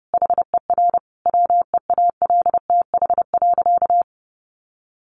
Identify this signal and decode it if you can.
Morse code, heard as a keyed tone, about 30 wpm, 700 Hz. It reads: HER WEALTH.
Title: Sentences from Top 1500 Words 30wpm